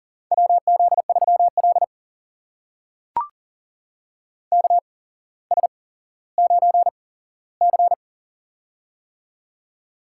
Morse code, heard as a keyed tone, about 38 wpm, spaced as for 17 wpm, 700 Hz. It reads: WZ3L KS9C
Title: USA Call Signs - 4 Characters - 8x Character Spacing - 40wpm